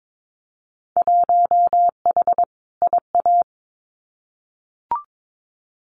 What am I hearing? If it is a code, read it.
Morse code, heard as a keyed tone, about 22 wpm, 700 Hz. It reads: 1H IA
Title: Winter Field Day Exchanges 22wpm